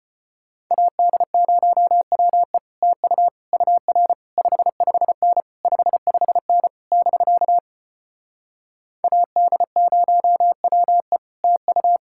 Morse code, heard as a keyed tone, about 33 wpm, 700 Hz. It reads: AD0WE TU UR 55N 55N <BK> AD0WE TU